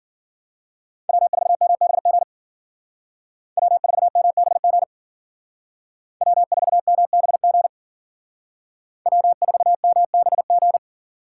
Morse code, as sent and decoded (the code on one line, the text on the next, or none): .-- ....- -- -... --. / .-- ....- -- -... --. / .-- ....- -- -... --. / .-- ....- -- -... --.
W4MBG W4MBG W4MBG W4MBG